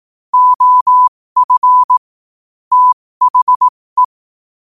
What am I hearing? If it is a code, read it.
Morse code, heard as a keyed tone, about 18 wpm, spaced as for 15 wpm, 1000 Hz. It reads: OF THE
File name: qotdfiles / 15wpm